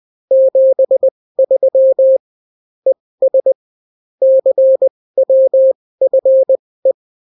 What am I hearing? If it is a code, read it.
Morse code, heard as a keyed tone, about 20 wpm, spaced as for 16 wpm, 530 Hz. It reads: 73 ES CWFE